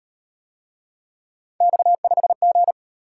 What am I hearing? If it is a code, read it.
Morse code, heard as a keyed tone, about 38 wpm, 700 Hz. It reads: X5G